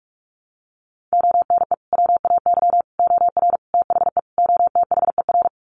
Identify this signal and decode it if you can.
Morse code, heard as a keyed tone, about 47 wpm, 700 Hz. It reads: ONE WAY OR THE OTHER
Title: Sets of 5 Words 45wpm